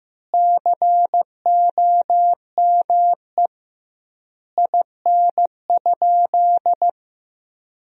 Morse code, heard as a keyed tone, about 15 wpm, 700 Hz. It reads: COME IN?
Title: 2x Word-Spacing - Sentences from Top 100 Words - 15wpm